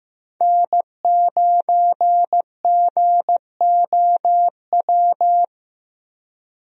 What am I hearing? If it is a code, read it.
Morse code, heard as a keyed tone, about 16 wpm, 700 Hz. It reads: N9GOW
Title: USA Call Signs - 5 Characters - Repeated 2x - 15wpm